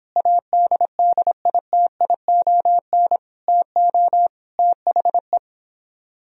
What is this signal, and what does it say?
Morse code, heard as a keyed tone, about 29 wpm, 700 Hz. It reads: ADDITION TO THE